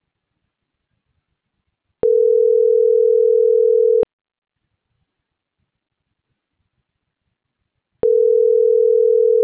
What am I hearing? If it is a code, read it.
Ringback tone (North America), 440 + 480 Hz.